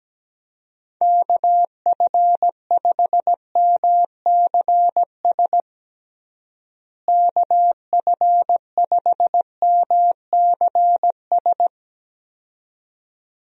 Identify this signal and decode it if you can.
Morse code, heard as a keyed tone, about 17 wpm, 700 Hz. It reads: KF5MCS KF5MCS
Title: USA Call Signs - 6 Characters - Repeated 2x - 17wpm